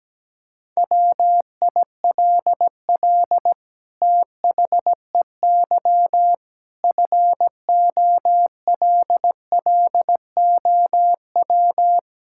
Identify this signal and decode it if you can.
Morse code, heard as a keyed tone, about 17 wpm, 700 Hz. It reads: WILL THEY FOLLOW
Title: Sentences from Top 1600 Words 17wpm